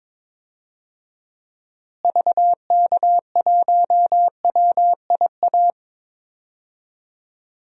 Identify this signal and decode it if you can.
Morse code, heard as a keyed tone, about 21 wpm, 700 Hz. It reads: VK1WIA
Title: DXCC-100 Call Signs - 6 Characters - Repeated 3x - 22wpm